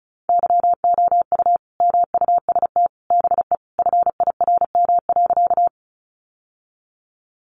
Morse code, heard as a keyed tone, about 34 wpm, 700 Hz. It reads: YOU MUST BE FIRM.
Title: Sentences from Top 1400 Words 35wpm